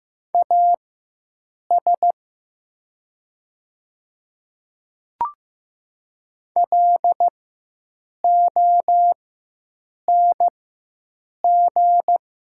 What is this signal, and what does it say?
Morse code, heard as a keyed tone, about 15 wpm, spaced as for 7 wpm, 700 Hz. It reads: AS LONG
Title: 4x Character Spacing - Top 100 Words - 15wpm